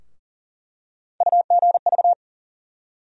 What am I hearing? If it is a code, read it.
Morse code, heard as a keyed tone, about 40 wpm, 700 Hz. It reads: UGV
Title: USA Call Sign Postfixes - 3 Letters 40wpm